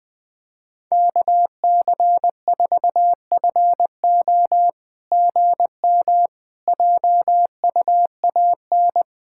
Morse code, heard as a keyed tone, about 21 wpm, 700 Hz. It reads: KC4FO GM JUAN
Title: SOTA QSO Practice 20wpm